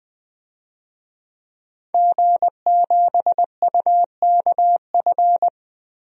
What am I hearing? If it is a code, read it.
Morse code, heard as a keyed tone, about 20 wpm, 700 Hz. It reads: G7UKF
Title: DXCC-100 Call Signs - 5 Characters - Repeated 2x - 20wpm